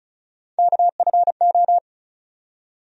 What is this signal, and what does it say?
Morse code, heard as a keyed tone, about 36 wpm, 700 Hz. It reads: KFO
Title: USA Call Sign Postfixes - 3 Letters 35wpm